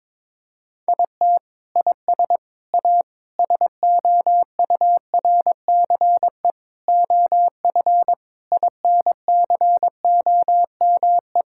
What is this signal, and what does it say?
Morse code, heard as a keyed tone, about 22 wpm, 700 Hz. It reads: IT IS A SOURCE OF INCOME
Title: Sentences from Top 1000 Words - Review Part 5 - 22wpm